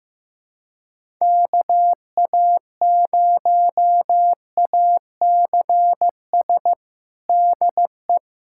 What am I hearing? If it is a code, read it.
Morse code, heard as a keyed tone, about 15 wpm, 700 Hz. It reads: KA0ACS DE
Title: USA Callsign DE Callsign 15wpm